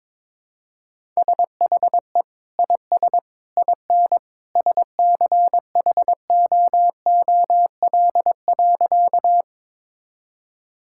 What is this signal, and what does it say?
Morse code, heard as a keyed tone, about 22 wpm, 700 Hz. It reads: SHE IS IN SCHOOL.
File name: practice / 22wpm / Sentences with 2-Word Phrases - 22wpm